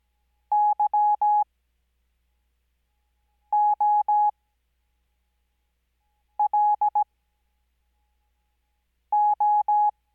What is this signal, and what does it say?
Morse code, heard as a keyed tone, about 17 wpm, spaced as for 4 wpm, 840 Hz. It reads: YOLO
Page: Farnsworth Spacing: 15WPM Character speed Spaced at 5WPM.